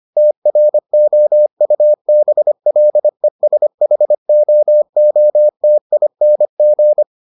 Morse code, heard as a keyed tone, about 25 wpm, 600 Hz. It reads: TROUBLESHOOTING